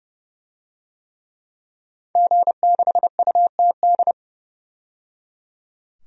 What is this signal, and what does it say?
Morse code, heard as a keyed tone, about 30 wpm, 700 Hz. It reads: G6UTD
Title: DXCC-100 Call Signs - 5 Characters - 30wpm